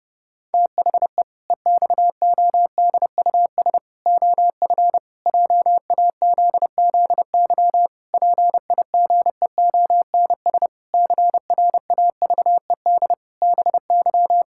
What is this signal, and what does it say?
Morse code, heard as a keyed tone, about 31 wpm, 700 Hz. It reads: THE EXODUS OF JAZZY PIGEONS CRAVED BY
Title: Pangram Challenge v2 - 30wpm